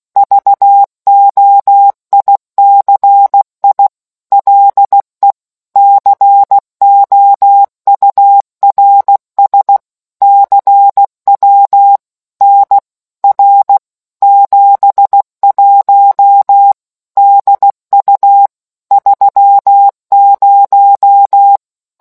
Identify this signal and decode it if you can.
Morse code, heard as a keyed tone, about 16 wpm, 800 Hz. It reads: VOICI LE COURS CW NR 71 DU 30